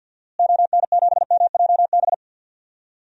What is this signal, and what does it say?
Morse code, heard as a keyed tone, about 49 wpm, 700 Hz. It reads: ON7MJB
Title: DXCC-100 Call Signs - 6 Characters - Repeated 3x - 50wpm